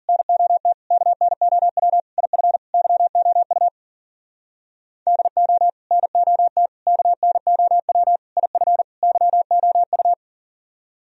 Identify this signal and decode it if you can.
Morse code, heard as a keyed tone, about 47 wpm, 700 Hz. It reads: NOT KNOW IF YOU DO NOT KNOW IF YOU
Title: Speed-Racing - Sets of 5 Words 40wpm